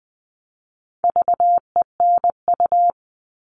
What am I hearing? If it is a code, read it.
Morse code, heard as a keyed tone, about 20 wpm, 700 Hz. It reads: VENU
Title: All Cut Numbers - 4-Digit 20wpm